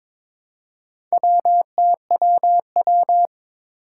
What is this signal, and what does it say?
Morse code, heard as a keyed tone, about 22 wpm, 700 Hz. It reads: WTWW
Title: Four Challenging Letter-Number Combinations 22wpm